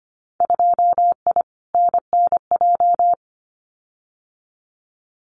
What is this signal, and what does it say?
Morse code, heard as a keyed tone, about 25 wpm, 700 Hz. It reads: 2I NNJ
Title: Winter Field Day Exchanges Encore 25wpm